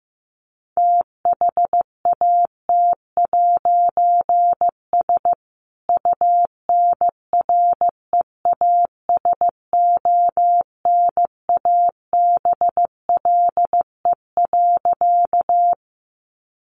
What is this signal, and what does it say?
Morse code, heard as a keyed tone, about 15 wpm, 700 Hz. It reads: THAT'S UNREASONABLE.
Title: A Sound Of Thunder 15wpm